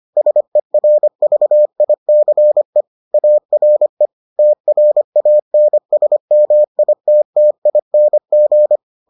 Morse code, heard as a keyed tone, about 25 wpm, 600 Hz. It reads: SERVICE ARE TRANSMITTING